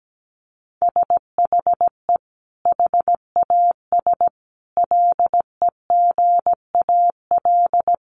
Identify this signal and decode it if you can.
Morse code, heard as a keyed tone, about 17 wpm, 700 Hz. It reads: SHE HAS LEGAL
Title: Sentences from Top 1200 Words 17wpm